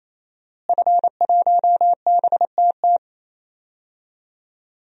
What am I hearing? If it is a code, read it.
Morse code, heard as a keyed tone, about 28 wpm, 700 Hz. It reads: F1BTT
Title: DXCC-100 Call Signs - 5 Characters - Repeated 3x - 28wpm